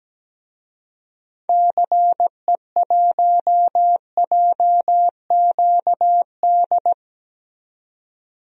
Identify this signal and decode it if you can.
Morse code, heard as a keyed tone, about 17 wpm, 700 Hz. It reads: CE1JQD
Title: DXCC-100 Call Signs - 6 Characters - 17wpm